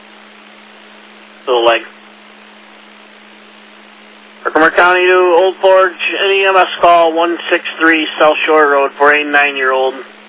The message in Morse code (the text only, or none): none